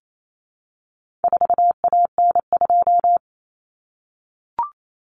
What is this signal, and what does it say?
Morse code, heard as a keyed tone, about 28 wpm, 700 Hz. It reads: 4AN2
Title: Common Cut Numbers - 4-Digit 28wpm